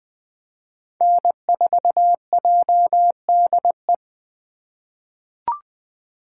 Morse code, heard as a keyed tone, about 21 wpm, 700 Hz. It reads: N4JDE
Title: USA Call Signs - 8x Character Spacing - 20wpm